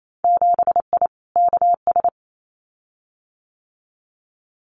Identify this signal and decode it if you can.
Morse code, heard as a keyed tone, about 28 wpm, 700 Hz. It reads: 7I KS
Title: Winter Field Day Exchanges Encore 28wpm